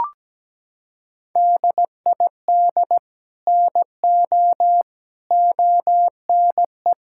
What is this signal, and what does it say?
Morse code, heard as a keyed tone, about 17 wpm, 700 Hz. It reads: DID NO ONE
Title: Sentences from Top 1600 Words 17wpm